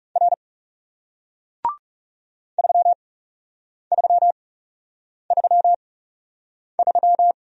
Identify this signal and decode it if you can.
Morse code, heard as a keyed tone, about 40 wpm, spaced as for 8 wpm, 700 Hz. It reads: R 3333